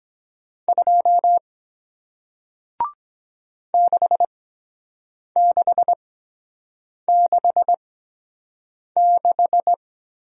Morse code, heard as a keyed tone, about 23 wpm, spaced as for 7 wpm, 700 Hz. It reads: 2 6666